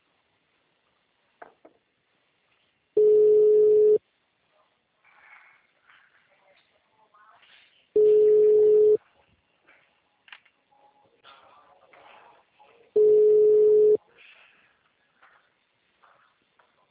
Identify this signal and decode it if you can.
Ringback tone (Europe, 425 Hz).